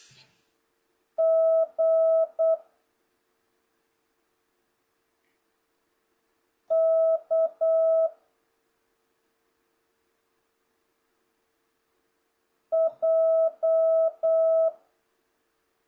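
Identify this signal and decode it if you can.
Morse code, heard as a keyed tone, about 8 wpm, 650 Hz. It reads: G K J
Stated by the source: Morse code lesson from JH8CBH's Zoom Morse training course (ZOOMモールス講習会), session 2 teaching letters G to K.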